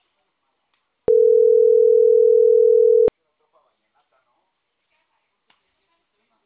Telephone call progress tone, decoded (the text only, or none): ringback tone